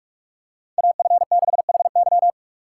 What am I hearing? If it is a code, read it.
Morse code, heard as a keyed tone, about 46 wpm, 700 Hz. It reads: AF6HY
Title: USA Call Signs - 5 Characters - 4x Character Spacing - 45wpm